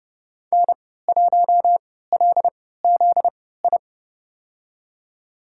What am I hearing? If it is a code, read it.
Morse code, heard as a keyed tone, about 32 wpm, spaced as for 17 wpm, 700 Hz. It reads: N1LZI
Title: USA Call Signs - 5 Characters - 3x Character Spacing - 30wpm